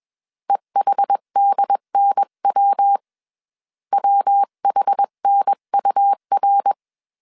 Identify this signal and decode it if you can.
Morse code, heard as a keyed tone, about 21 wpm, 800 Hz. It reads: EHDNW WHNUR
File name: CW / DET / 1 3 1500C S B 20wpm 1754